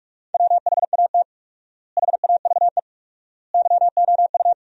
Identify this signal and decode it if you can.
Morse code, heard as a keyed tone, about 47 wpm, 700 Hz. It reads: WHAT HAVE YOU
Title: 4x Word-Spacing - Sentences from Top 400 Words - 45wpm